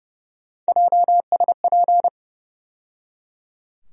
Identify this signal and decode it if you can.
Morse code, heard as a keyed tone, about 31 wpm, 700 Hz. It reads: JSP
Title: USA Call Sign Postfixes - 3 Letters 30wpm